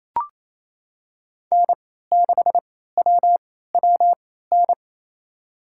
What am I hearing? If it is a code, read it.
Morse code, heard as a keyed tone, about 28 wpm, spaced as for 16 wpm, 700 Hz. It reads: N6WWN
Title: USA Call Signs - 3x Character Spacing - 28wpm